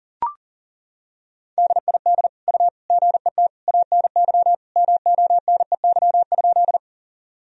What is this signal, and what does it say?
Morse code, heard as a keyed tone, about 40 wpm, 700 Hz. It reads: DID U GET ANY MONEY?
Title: Sentences from Top 400 Words and Abbreviations 40wpm